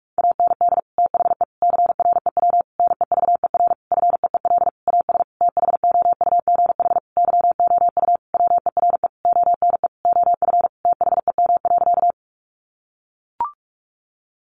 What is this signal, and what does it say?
Morse code, heard as a keyed tone, about 44 wpm, 700 Hz. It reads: AND THE CREW NEVER FEEL AS THOUGH YOU WERE ONE OF THEM.
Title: Sentences from Top 1500 Words 45wpm